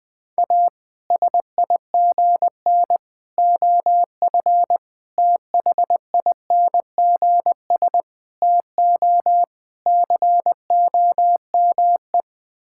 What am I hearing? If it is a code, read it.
Morse code, heard as a keyed tone, about 20 wpm, 700 Hz. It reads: A SIGN OF THINGS TO COME
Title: Sets of 6 Words - 20wpm